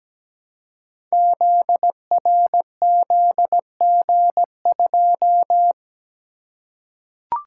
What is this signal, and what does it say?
Morse code, heard as a keyed tone, about 17 wpm, 700 Hz. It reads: ZRZG2
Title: Code Groups - 5 Letter:Numbers - 17wpm